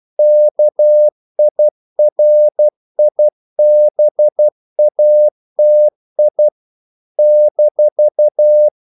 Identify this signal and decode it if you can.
Morse code, heard as a keyed tone, about 12 wpm, 600 Hz. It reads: KIRIBATI -
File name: CW / Practice / NationalCapitals12